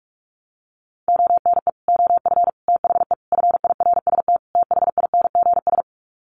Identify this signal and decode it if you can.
Morse code, heard as a keyed tone, about 44 wpm, 700 Hz. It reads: ONE OF THE FIRST THINGS
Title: Sets of 5 Words Encore 45wpm